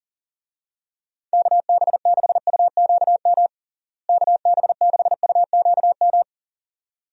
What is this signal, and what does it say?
Morse code, heard as a keyed tone, about 40 wpm, 700 Hz. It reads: KB6UQM KB6UQM
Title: USA Call Signs - 6 Characters - Repeated 2x - 40wpm